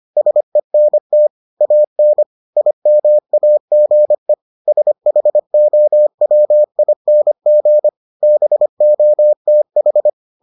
Morse code, heard as a keyed tone, about 25 wpm, 600 Hz. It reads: SENT AN IMAGE SHOWING BOTH